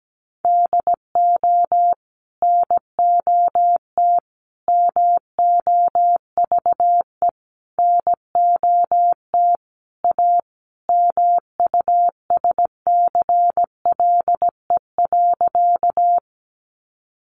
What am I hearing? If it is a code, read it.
Morse code, heard as a keyed tone, about 17 wpm, 700 Hz. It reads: DO NOT MOVE NOT A MUSCLE.
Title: Sentences from Top 1300 Words 17wpm